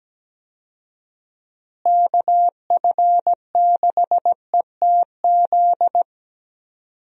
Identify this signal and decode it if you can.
Morse code, heard as a keyed tone, about 17 wpm, 700 Hz. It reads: KF6ETZ
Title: USA Call Signs - 6 Characters - Repeated 3x - 17wpm